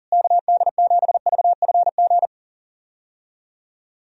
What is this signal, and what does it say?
Morse code, heard as a keyed tone, about 40 wpm, 700 Hz. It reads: KD7VFG
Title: USA Call Signs - 6 Characters - Repeated 2x - 40wpm